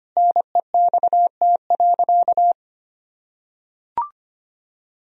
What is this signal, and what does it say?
Morse code, heard as a keyed tone, about 25 wpm, 700 Hz. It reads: NEXT.
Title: Sentences from Top 1000 Words - Review Part 2 - 25wpm